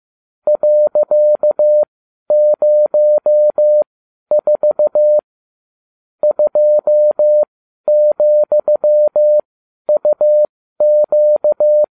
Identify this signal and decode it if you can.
Morse code, heard as a keyed tone, about 15 wpm, spaced as for 11 wpm, 600 Hz. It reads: .04 2,UQ